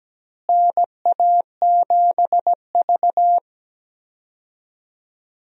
Morse code, heard as a keyed tone, about 17 wpm, 700 Hz. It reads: NA7V
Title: USA Call Signs - 4 Characters - Repeated 2x - 17wpm